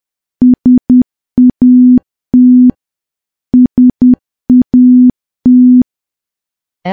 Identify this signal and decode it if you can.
Morse code, heard as a keyed tone, about 10 wpm, 250 Hz. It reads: SAT SAT
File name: morse / morse-training-10WPM